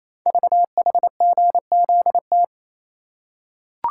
Morse code, heard as a keyed tone, about 28 wpm, 700 Hz. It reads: VHGZT
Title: Code Groups - 5 Letters - 28wpm